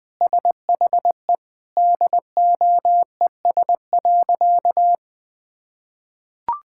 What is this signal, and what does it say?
Morse code, heard as a keyed tone, about 20 wpm, 700 Hz. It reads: SHE DOES.
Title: Sentences with 5-Word Phrases - Spoken Beginning - 20wpm